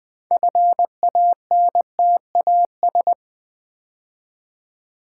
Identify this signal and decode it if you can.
Morse code, heard as a keyed tone, about 20 wpm, 700 Hz. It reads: FANTAS
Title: Build A Word - 3 Letters At A Time - 9-Letter Words - 20wpm